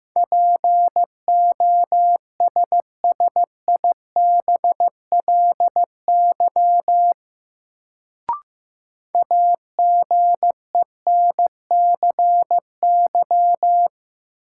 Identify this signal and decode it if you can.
Morse code, heard as a keyed tone, about 15 wpm, 700 Hz. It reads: POSSIBLY AGENCY
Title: Sentences from Top 1300 Words - New Word Drill Alt - 15wpm